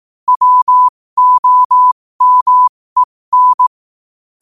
Morse code, heard as a keyed tone, about 18 wpm, spaced as for 15 wpm, 1000 Hz. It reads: WOMEN
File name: 15wpm